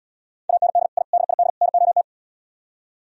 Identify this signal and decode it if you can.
Morse code, heard as a keyed tone, about 77 wpm, 700 Hz. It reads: AND I DID NOT